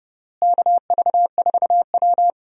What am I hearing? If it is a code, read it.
Morse code, heard as a keyed tone, about 31 wpm, 700 Hz. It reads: KV4W
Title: USA Call Signs - 4 Characters - Repeated 3x - 30wpm